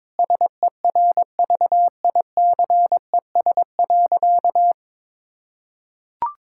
Morse code, heard as a keyed tone, about 22 wpm, 700 Hz. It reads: SERVICES.
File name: practice / 22wpm / Sentences with 4-Word Phrases - 22wpm